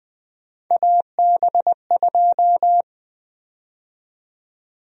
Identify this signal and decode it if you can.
Morse code, heard as a keyed tone, about 20 wpm, 700 Hz. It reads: AB2
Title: USA Call Sign Prefixes - Repeated 2x - 20wpm